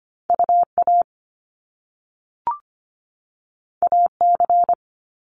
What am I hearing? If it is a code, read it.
Morse code, heard as a keyed tone, about 25 wpm, 700 Hz. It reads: UA AC
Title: Top Two Letter Combinations in Words 25wpm